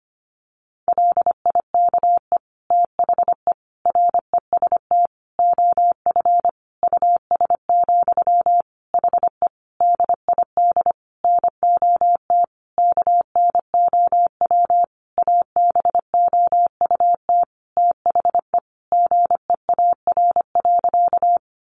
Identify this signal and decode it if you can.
Morse code, heard as a keyed tone, about 25 wpm, 700 Hz. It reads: LIKE THE REST OF US, HE DID NOT KNOW ABOUT THE GEAR.
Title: Sentences from Top 1400 Words 25wpm